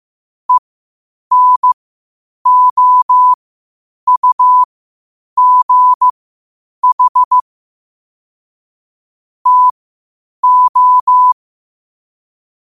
Morse code, heard as a keyed tone, about 15 wpm, spaced as for 8 wpm, 1000 Hz. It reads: ENOUGH TO
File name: qotdfiles / 8wpm